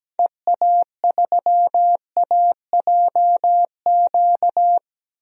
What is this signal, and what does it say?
Morse code, heard as a keyed tone, about 17 wpm, 700 Hz. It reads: EA3AJQ
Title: DXCC-100 Call Signs - 6 Characters - 17wpm